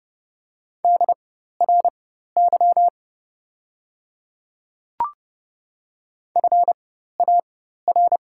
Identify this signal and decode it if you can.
Morse code, heard as a keyed tone, about 30 wpm, spaced as for 14 wpm, 700 Hz. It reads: DRY FAR
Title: 4x Character Spacing - Top 500 Words - 30wpm